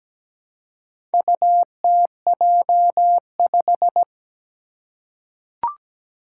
Morse code, heard as a keyed tone, about 17 wpm, 700 Hz. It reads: UTJ5
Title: Four Challenging Letter-Number Combinations 17wpm